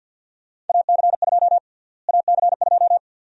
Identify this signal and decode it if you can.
Morse code, heard as a keyed tone, about 53 wpm, 700 Hz. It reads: AC2 AC2
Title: USA Call Sign Prefixes - Repeated 2x - 50wpm